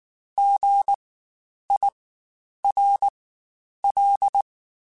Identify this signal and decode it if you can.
Morse code, heard as a keyed tone, about 19 wpm, spaced as for 9 wpm, 800 Hz. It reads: GIRL
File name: Spaced Alphas / Spaced Alphas mp3s / Gs sp @18